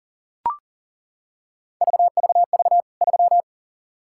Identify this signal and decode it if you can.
Morse code, heard as a keyed tone, about 40 wpm, 700 Hz. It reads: VVV 3
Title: K1USN SST QSO Practice 40wpm